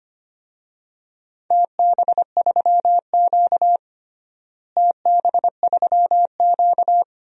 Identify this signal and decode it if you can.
Morse code, heard as a keyed tone, about 25 wpm, 700 Hz. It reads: TB3Q TB3Q
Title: DXCC-100 Call Signs - 4 Characters - Repeated 2x - 25wpm